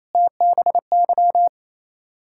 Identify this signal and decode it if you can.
Morse code, heard as a keyed tone, about 29 wpm, 690 Hz. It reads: TBY
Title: Ontario QSO Party Exchange 28wpm